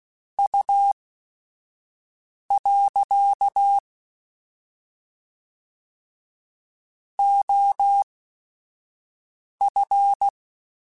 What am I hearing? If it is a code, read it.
Morse code, heard as a keyed tone, about 16 wpm, spaced as for 5 wpm, 800 Hz. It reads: U. OF
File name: QSOs / QSOs-mp3s / WA7QOX 1@5